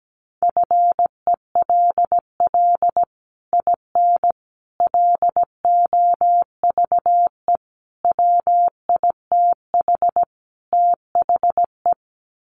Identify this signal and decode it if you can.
Morse code, heard as a keyed tone, about 17 wpm, 700 Hz. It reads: FELL IN LOVE WITH THE
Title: Sets of 5 Words Encore2 17wpm